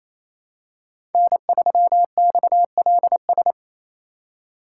Morse code, heard as a keyed tone, about 28 wpm, 700 Hz. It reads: N3XLS
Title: K1USN SST QSO Practice - 28wpm